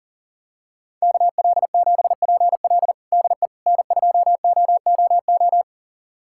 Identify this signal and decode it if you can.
Morse code, heard as a keyed tone, about 40 wpm, 700 Hz. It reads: KL7PL DE N2OOO
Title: USA Callsign DE Callsign - 5 Characters - 40wpm